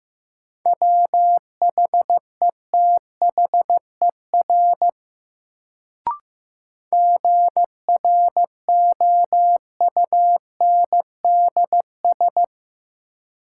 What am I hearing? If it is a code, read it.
Morse code, heard as a keyed tone, about 15 wpm, 700 Hz. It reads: WHETHER GROUNDS
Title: Sentences from Top 1300 Words - New Word Drill Alt - 15wpm